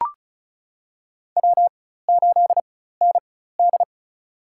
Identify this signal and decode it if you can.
Morse code, heard as a keyed tone, about 36 wpm, spaced as for 16 wpm, 700 Hz. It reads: W8ND
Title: USA Call Signs - 4 Characters - 4x Character Spacing - 35wpm